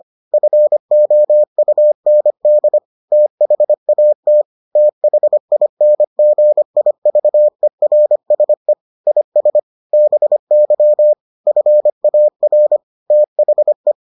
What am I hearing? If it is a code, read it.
Morse code, heard as a keyed tone, about 25 wpm, 600 Hz. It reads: FOUND THAT THINGIVERSE IS BY FAR THE